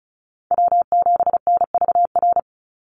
Morse code, heard as a keyed tone, about 34 wpm, 700 Hz. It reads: W7NVR
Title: USA Call Signs 35wpm